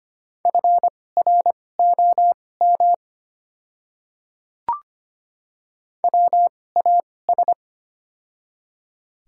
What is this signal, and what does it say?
Morse code, heard as a keyed tone, about 25 wpm, spaced as for 18 wpm, 700 Hz. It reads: FROM WAS
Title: 2x Character Spacing - Top 100 Words - 25wpm